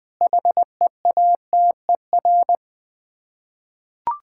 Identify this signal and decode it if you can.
Morse code, heard as a keyed tone, about 20 wpm, 700 Hz. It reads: HEATER
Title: Difficult Words with Hints - 20wpm